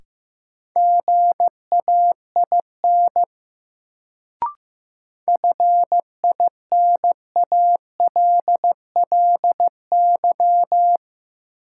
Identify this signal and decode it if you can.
Morse code, heard as a keyed tone, about 15 wpm, 700 Hz. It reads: GAIN FINALLY
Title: Sentences from Top 1000 Words - New Word Drill - 15wpm